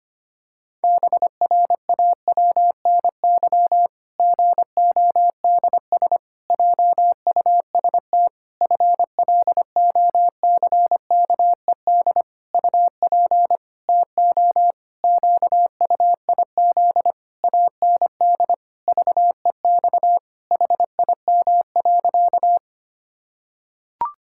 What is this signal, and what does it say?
Morse code, heard as a keyed tone, about 25 wpm, 700 Hz. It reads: BRAWNY GODS JUST FLOCKED UP TO QUIZ AND VEX HIM.
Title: Pangram Challenge v2 - 25wpm